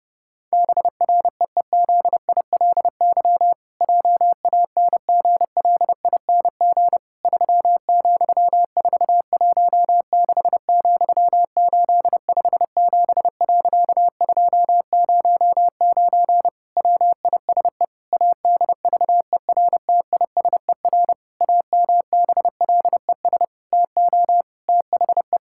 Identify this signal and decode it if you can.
Morse code, heard as a keyed tone, about 30 wpm, 700 Hz. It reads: BREEZILY JANGLING 3,416,857.209 WISE ADVERTISER AMBLES TO THE
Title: Pangram Challenge v2 - 30wpm